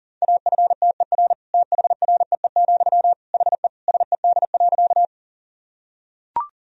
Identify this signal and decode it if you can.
Morse code, heard as a keyed tone, about 40 wpm, 700 Hz. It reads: AFTER THREE, HE SED.
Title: Sentences from Top 200 Words and Abbreviations 40wpm